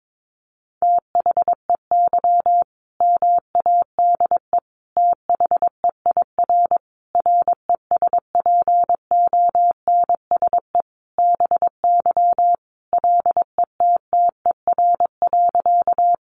Morse code, heard as a keyed tone, about 22 wpm, 700 Hz. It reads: THEY MADE THEIR RESPONSE BY LETTER.
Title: Sentences from Top 1400 Words 22wpm